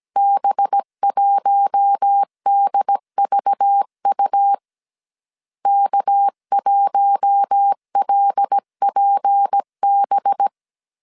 Morse code, heard as a keyed tone, about 17 wpm, 800 Hz. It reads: B1DVU K1LPB
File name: 16wpm